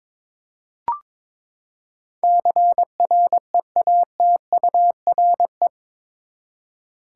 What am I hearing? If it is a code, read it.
Morse code, heard as a keyed tone, about 22 wpm, 700 Hz. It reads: CREATURE
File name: practice / 22wpm / Eight Letter Words - 22wpm